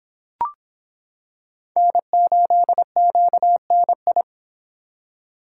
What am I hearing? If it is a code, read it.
Morse code, heard as a keyed tone, about 26 wpm, 700 Hz. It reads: N8QNI